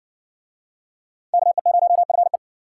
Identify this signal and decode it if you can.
Morse code, heard as a keyed tone, about 55 wpm, 700 Hz. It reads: KE0FE